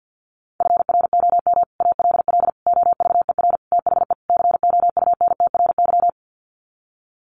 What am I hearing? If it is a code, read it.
Morse code, heard as a keyed tone, about 50 wpm, 700 Hz. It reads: FROM ALL OVER THE COUNTRY
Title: Sets of 5 Words Encore 50wpm